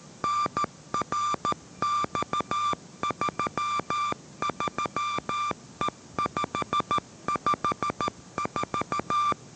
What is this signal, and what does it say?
Morse code, heard as a keyed tone, about 13 wpm, 1180 Hz. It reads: NRX33E554
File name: 14wpm